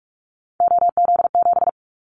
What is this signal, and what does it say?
Morse code, heard as a keyed tone, about 44 wpm, 700 Hz. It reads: OZ7
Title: Callsign Prefixes for Top 100 DXCC - 45wpm